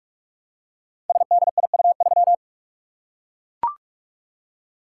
Morse code, heard as a keyed tone, about 44 wpm, 700 Hz. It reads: SDIU3
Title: Code Groups - 5 Letter:Numbers - 45wpm